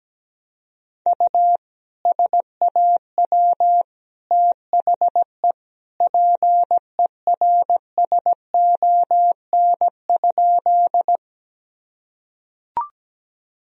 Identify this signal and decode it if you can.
Morse code, heard as a keyed tone, about 17 wpm, 700 Hz. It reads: U SAW THE PERSON?
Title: Sentences from Top 400 Words and Abbreviations 17wpm